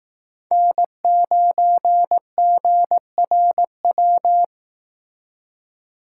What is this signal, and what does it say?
Morse code, heard as a keyed tone, about 18 wpm, 700 Hz. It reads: N9GRW
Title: Speed Racing - USA 1x3 Call Signs 15wpm